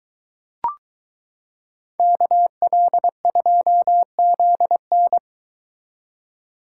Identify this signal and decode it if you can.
Morse code, heard as a keyed tone, about 22 wpm, 700 Hz. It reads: KL2ZN